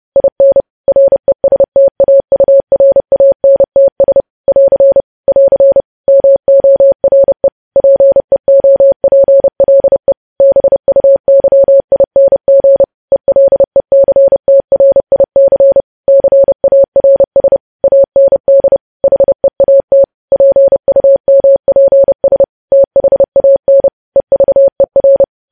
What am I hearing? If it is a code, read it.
Morse code, heard as a keyed tone, about 30 wpm, 550 Hz. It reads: IN RESTAURANTS <AR> <AR> MORE PEOPLE BUYING ELECTRIC CARS AND HEAT PUMPS THAN EVER
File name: podcast / News headlines-1x-30x30